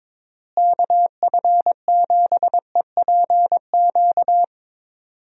Morse code, heard as a keyed tone, about 22 wpm, 700 Hz. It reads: KF7EPQ
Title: USA Call Signs - 6 Characters - Repeated 2x - 22wpm